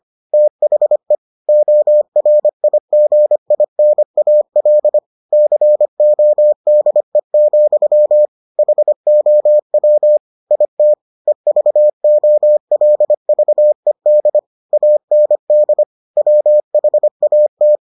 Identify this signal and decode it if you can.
Morse code, heard as a keyed tone, about 25 wpm, 600 Hz. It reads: THE ORIGINAL CODE, HOW IT EVOLVED AND WHAT